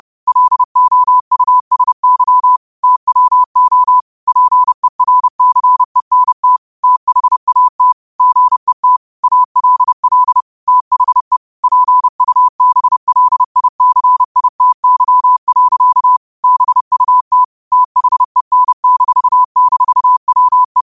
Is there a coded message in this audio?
Yes: Morse code from 0.3 to 20.8 s, heard as a keyed tone, about 30 wpm, 1000 Hz. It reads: LOUSY TWO PERCENT THAT GET ALL THE PUBLICITY. BUT THEN--WE